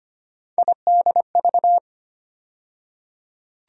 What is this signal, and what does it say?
Morse code, heard as a keyed tone, about 25 wpm, 700 Hz. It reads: IDV
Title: USA Call Sign Postfixes - 3 Letters 25wpm